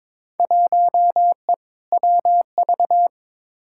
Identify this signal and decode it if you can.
Morse code, heard as a keyed tone, about 22 wpm, 700 Hz. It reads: 1E WV
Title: Field Day Exchanges Encore - 22wpm